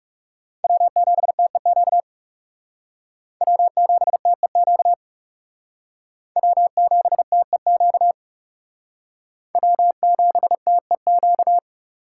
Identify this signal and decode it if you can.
Morse code, heard as a keyed tone, about 37 wpm, 700 Hz. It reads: W7TEQ W7TEQ W7TEQ W7TEQ
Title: Speed Racing - USA 1x3 Call Signs 30wpm